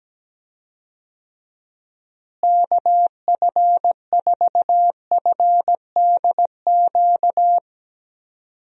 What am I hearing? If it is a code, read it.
Morse code, heard as a keyed tone, about 17 wpm, 700 Hz. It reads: KF4FDQ
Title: USA Call Signs - 6 Characters - Repeated 3x - 17wpm